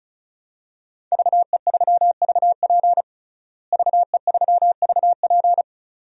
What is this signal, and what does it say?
Morse code, heard as a keyed tone, about 36 wpm, 700 Hz. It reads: VE3VP VE3VP
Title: DXCC-100 Call Signs - 5 Characters - Repeated 2x - 35wpm